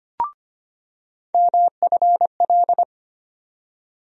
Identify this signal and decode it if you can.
Morse code, heard as a keyed tone, about 25 wpm, 700 Hz. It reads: MFL
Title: USA Call Sign Postfixes - 3 Letters 25wpm